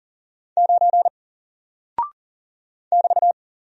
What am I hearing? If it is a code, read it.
Morse code, heard as a keyed tone, about 40 wpm, 700 Hz. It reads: 9 <BT>
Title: Single Character (Letter:Number:Punctuation:Prosign) Alt 40wpm